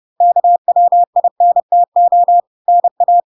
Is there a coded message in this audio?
Yes: Morse code from 0.2 to 3.2 s, heard as a keyed tone, about 30 wpm, 700 Hz. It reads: KWINTO NA